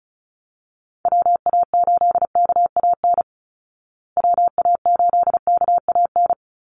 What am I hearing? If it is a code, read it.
Morse code, heard as a keyed tone, about 34 wpm, 700 Hz. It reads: WA8KAN WA8KAN
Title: Call Signs Repeated 3x - 35wpm